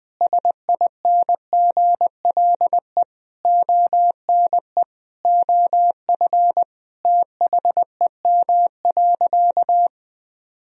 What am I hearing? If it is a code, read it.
Morse code, heard as a keyed tone, about 20 wpm, 700 Hz. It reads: SINGLE ONE OF THEM.
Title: Sentences from Top 1000 Words - Review Part 4 - 20wpm